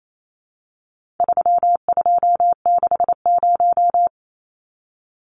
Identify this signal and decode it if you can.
Morse code, heard as a keyed tone, about 28 wpm, 700 Hz. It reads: 3260
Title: Numbers - 4-Digit 28wpm